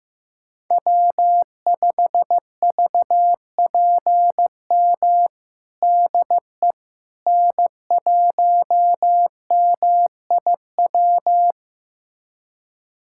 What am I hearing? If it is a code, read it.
Morse code, heard as a keyed tone, about 16 wpm, 700 Hz. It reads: W5VPM DE N1MIW
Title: USA Callsign DE Callsign - 5 Characters - 15wpm